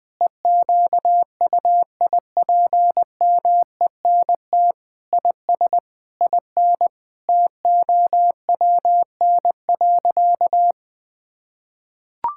Morse code, heard as a keyed tone, about 20 wpm, 700 Hz. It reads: EQUIPMENT IS IN TOWN.
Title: Sentences from Top 1000 Words - Review Part 5 - 20wpm